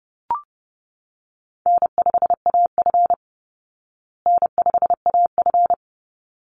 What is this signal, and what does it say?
Morse code, heard as a keyed tone, about 30 wpm, 700 Hz. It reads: N5AF N5AF
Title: Easy Call Signs Repeated 2x - 30wpm